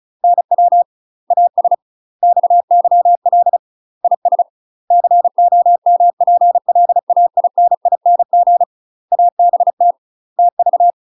Morse code, heard as a keyed tone, about 36 wpm, 700 Hz. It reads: NW AS XYL IS COMPLAINING ABT TV
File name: UK MASTER QSOs TRAINING / 35WPM / UK MASTER QSOs 0026